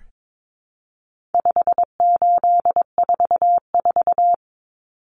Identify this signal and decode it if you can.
Morse code, heard as a keyed tone, about 22 wpm, 700 Hz. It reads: 5844
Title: Common Cut Numbers - 4-Digit 22wpm